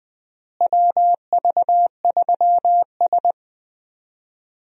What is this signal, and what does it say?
Morse code, heard as a keyed tone, about 20 wpm, 700 Hz. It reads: WV3S
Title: USA Call Signs - 4 Characters - Repeated 3x - 20wpm